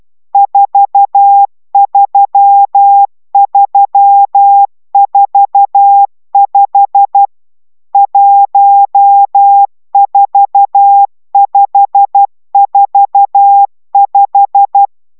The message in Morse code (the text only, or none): 43345 14545